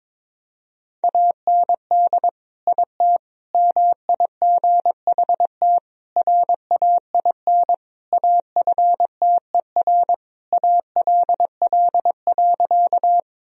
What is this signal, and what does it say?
Morse code, heard as a keyed tone, about 22 wpm, 700 Hz. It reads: AND IT MIGHT RAIN AFTER ALL.
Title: Sentences from Top 1000 Words - Review Part 1 - 22wpm